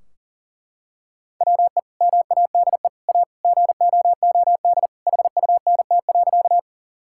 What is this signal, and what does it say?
Morse code, heard as a keyed tone, about 40 wpm, 700 Hz. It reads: WE MADE A GOOD HUNT.
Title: Sentences from Top 1600 Words 40wpm